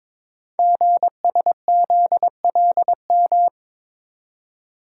Morse code, heard as a keyed tone, about 22 wpm, 700 Hz. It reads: GSZLM
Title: Code Groups - 5 Letters - 22wpm